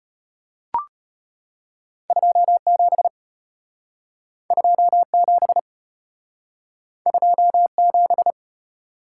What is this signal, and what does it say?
Morse code, heard as a keyed tone, about 34 wpm, 700 Hz. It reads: 27 27 27